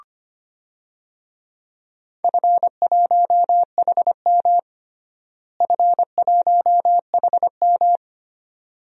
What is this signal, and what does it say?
Morse code, heard as a keyed tone, about 25 wpm, 700 Hz. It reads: F1HM F1HM
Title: DXCC-100 Call Signs - 4 Characters - Repeated 2x - 25wpm